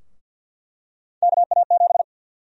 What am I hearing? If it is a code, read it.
Morse code, heard as a keyed tone, about 53 wpm, 700 Hz. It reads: KA7
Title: USA Call Sign Prefixes - Repeated 2x - 50wpm